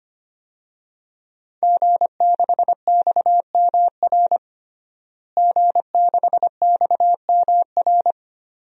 Morse code, heard as a keyed tone, about 25 wpm, 700 Hz. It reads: G6XMR G6XMR
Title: DXCC-100 Call Signs - 5 Characters - Repeated 2x - 25wpm